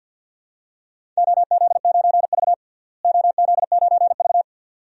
Morse code, heard as a keyed tone, about 51 wpm, 700 Hz. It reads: OZ9V OZ9V
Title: DXCC-100 Call Signs - 4 Characters - Repeated 2x - 50wpm